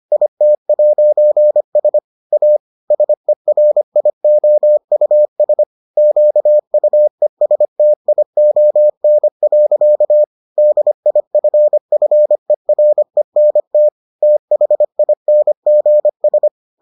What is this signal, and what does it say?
Morse code, heard as a keyed tone, about 25 wpm, 600 Hz. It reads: IT'S A SERIOUS QUESTION. DIFFERENT THINGS